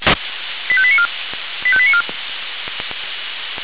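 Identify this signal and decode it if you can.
Five-tone selective call (ZVEI), heard as quick runs of five tones, tones of about 70 ms. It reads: 85894 85894
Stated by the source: ZVEI five-tone selective call received by SDR.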